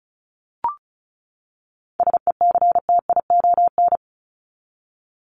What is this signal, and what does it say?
Morse code, heard as a keyed tone, about 34 wpm, 700 Hz. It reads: SECTION
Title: Top 501-750 Words 35wpm